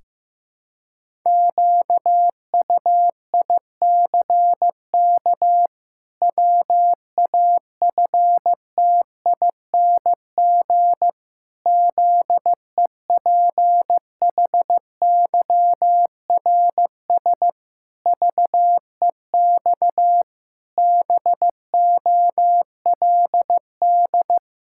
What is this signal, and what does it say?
Morse code, heard as a keyed tone, about 16 wpm, 700 Hz. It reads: QUICK WAFTING ZEPHYRS VEX BOLD
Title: Pangram Challenge v2 - 15wpm